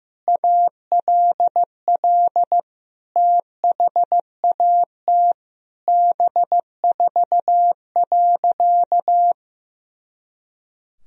Morse code, heard as a keyed tone, about 15 wpm, 700 Hz. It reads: ALL THAT B4.
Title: Sentences from Top 400 Words and Abbreviations 15wpm